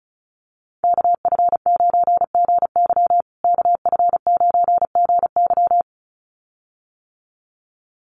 Morse code, heard as a keyed tone, about 34 wpm, 700 Hz. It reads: KF9GY KF9GY
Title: USA Call Signs Repeated 2x 35wpm